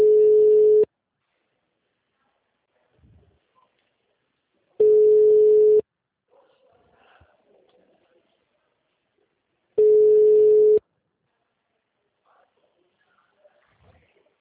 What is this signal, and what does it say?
Ringback tone (Europe, 425 Hz).